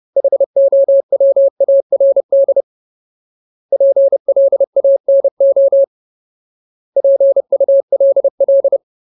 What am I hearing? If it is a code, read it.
Morse code, heard as a keyed tone, about 30 wpm, 550 Hz. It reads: HOWARD PLANO PULL
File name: Ultimate Word List / 30WPM Ultimate